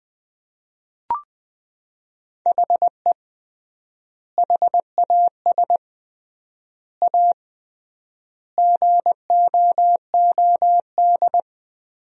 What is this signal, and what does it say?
Morse code, heard as a keyed tone, about 20 wpm, 700 Hz. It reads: HE HAS A GOOD
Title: 3x Word-Spacing - Sentences from Top 500 Words - 20wpm